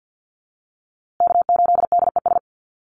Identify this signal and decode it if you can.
Morse code, heard as a keyed tone, about 50 wpm, 700 Hz. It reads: K8DES
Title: USA Call Signs Repeated 3x 50wpm